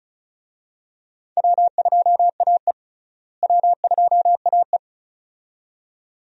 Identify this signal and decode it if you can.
Morse code, heard as a keyed tone, about 36 wpm, 700 Hz. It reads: W2AE W2AE
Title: USA Call Signs - 4 Characters - Repeated 2x - 35wpm